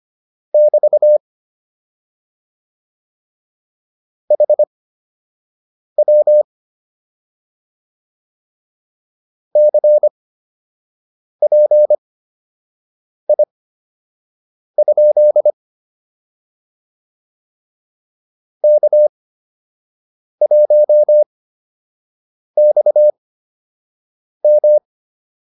Morse code, heard as a keyed tone, about 25 wpm, spaced as for 6 wpm, 600 Hz. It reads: <BT> HW CPI? K1XM